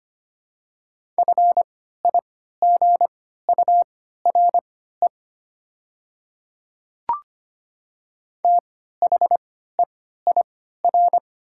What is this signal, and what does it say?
Morse code, heard as a keyed tone, about 25 wpm, spaced as for 14 wpm, 690 Hz. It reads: FIGURE THEIR
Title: 3x Character Spacing - Top 500 Words - 25wpm